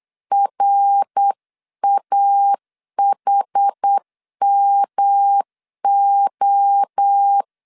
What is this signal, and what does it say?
Morse code, heard as a keyed tone, about 8 wpm, 800 Hz. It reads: RAHMO
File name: CW / DET / 1 3 1500C S C 8wpm 4214